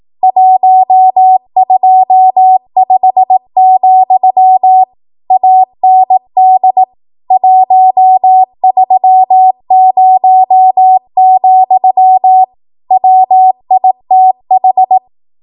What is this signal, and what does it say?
Morse code, heard as a keyed tone, about 18 wpm, 750 Hz. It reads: 125, AND 130, WITH
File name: Morse / 18WPM